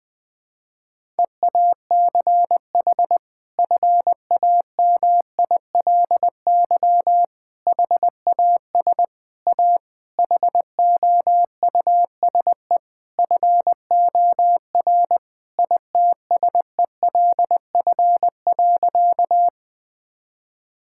Morse code, heard as a keyed tone, about 21 wpm, 700 Hz. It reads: EACH FAMILY HAS A HOUSE FOR ITSELF.
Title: Sentences from Top 1000 Words - Review Part 5 - 20wpm